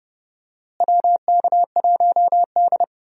Morse code, heard as a keyed tone, about 31 wpm, 700 Hz. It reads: WK1D
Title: USA Call Signs - 4 Characters - Repeated 2x - 30wpm